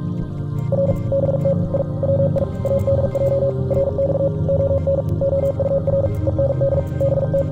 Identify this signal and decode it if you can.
Morse code, heard as a keyed tone, about 43 wpm, 550 Hz. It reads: R BT I WI ADJ RX ON OUR END BT